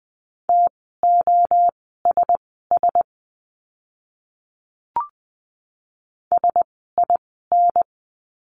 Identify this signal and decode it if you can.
Morse code, heard as a keyed tone, about 20 wpm, spaced as for 15 wpm, 700 Hz. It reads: TOSS SIN
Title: 036 Lesson 7f - Letter s - Words 20wpm